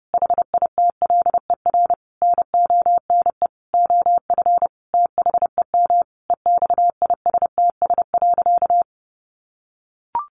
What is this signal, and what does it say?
Morse code, heard as a keyed tone, about 30 wpm, 700 Hz. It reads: HITLER NONE OF THEM EXISTS.
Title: A Sound Of Thunder 30wpm